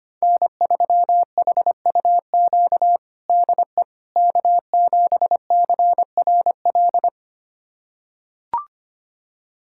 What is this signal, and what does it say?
Morse code, heard as a keyed tone, about 25 wpm, 700 Hz. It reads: N3HUQ DE K7CRL
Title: USA Callsign DE Callsign - 5 Characters - 25wpm